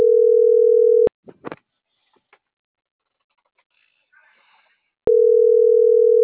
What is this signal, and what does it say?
Ringback tone (North America), 440 + 480 Hz.